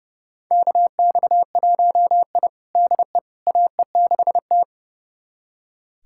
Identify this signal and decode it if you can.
Morse code, heard as a keyed tone, about 29 wpm, 700 Hz. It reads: KX1I DE AE6T
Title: USA Callsign DE Callsign - 4 Characters - 30wpm